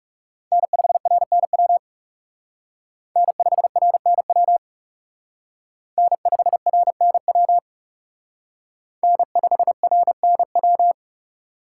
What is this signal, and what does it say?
Morse code, heard as a keyed tone, about 34 wpm, 700 Hz. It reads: N5RNW N5RNW N5RNW N5RNW
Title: Speed Racing - USA Call Signs 30wpm